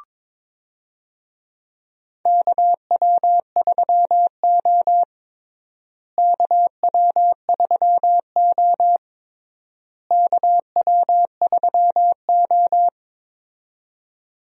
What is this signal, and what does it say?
Morse code, heard as a keyed tone, about 23 wpm, 700 Hz. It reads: KW3O KW3O KW3O
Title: USA Call Signs - 4 Characters - Repeated 3x - 22wpm